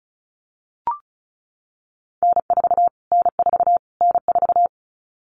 Morse code, heard as a keyed tone, about 34 wpm, 700 Hz. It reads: N4 N4 N4
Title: Callsign Prefixes for Top 25 DXCC - Repeated 3x - 35wpm